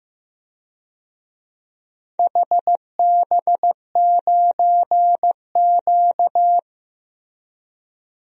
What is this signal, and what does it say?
Morse code, heard as a keyed tone, about 16 wpm, 700 Hz. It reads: HB9Q
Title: DXCC-100 Call Signs - 4 Characters - 15wpm